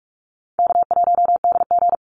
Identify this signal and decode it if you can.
Morse code, heard as a keyed tone, about 46 wpm, 700 Hz. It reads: K1DG
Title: Easy Call Signs - 45wpm